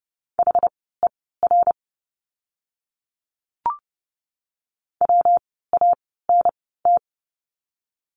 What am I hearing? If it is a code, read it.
Morse code, heard as a keyed tone, about 30 wpm, spaced as for 17 wpm, 700 Hz. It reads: HER WANT
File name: course / 30:10wpm / 115 Lesson 19f - Letter w - Words 30wpm